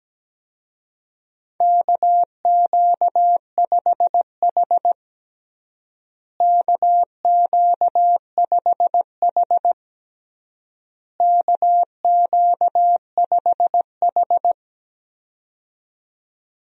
Morse code, heard as a keyed tone, about 17 wpm, 700 Hz. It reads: KQ5H KQ5H KQ5H
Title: USA Call Signs - 4 Characters - Repeated 3x - 17wpm